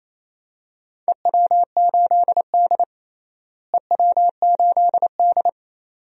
Morse code, heard as a keyed tone, about 28 wpm, 700 Hz. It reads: EW8D EW8D
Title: DXCC-100 Call Signs - 4 Characters - Repeated 2x - 28wpm